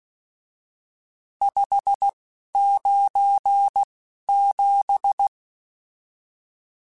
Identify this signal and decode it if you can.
Morse code, heard as a keyed tone, about 16 wpm, 800 Hz. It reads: 5 9 7
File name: RSTs / RST-mp3s / Various RST @10